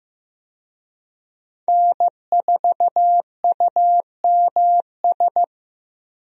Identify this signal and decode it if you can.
Morse code, heard as a keyed tone, about 15 wpm, 700 Hz. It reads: N4UMS
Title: USA Call Signs - 5 Characters - 15wpm